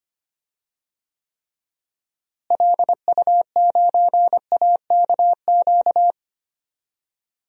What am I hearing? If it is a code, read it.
Morse code, heard as a keyed tone, about 25 wpm, 700 Hz. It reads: LU9AKQ
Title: DXCC-100 Call Signs - 6 Characters - Repeated 3x - 25wpm